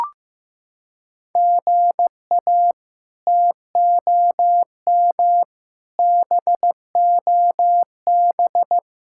Morse code, heard as a keyed tone, about 15 wpm, 700 Hz. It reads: GA TOM BOB
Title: K1USN SST QSO Practice - 15wpm